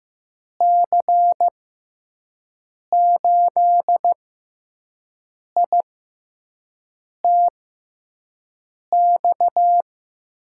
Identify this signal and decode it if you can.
Morse code, heard as a keyed tone, about 15 wpm, spaced as for 5 wpm, 700 Hz. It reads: C8ITX
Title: USA Call Signs - 6 Characters - 6x Character Spacing - 15wpm